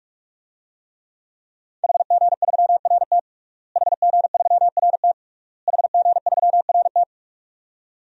Morse code, heard as a keyed tone, about 47 wpm, 700 Hz. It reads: HG3RT HG3RT HG3RT
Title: DXCC-100 Call Signs - 5 Characters - Repeated 3x - 45wpm